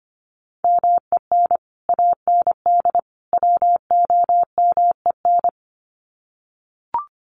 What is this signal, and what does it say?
Morse code, heard as a keyed tone, about 25 wpm, 700 Hz. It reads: MEN AND WOMEN
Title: Sets of 3 Words 25wpm